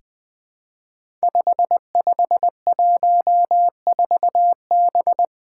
Morse code, heard as a keyed tone, about 20 wpm, 700 Hz. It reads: 5514B
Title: Five Challenging Letter-Number Combinations 20wpm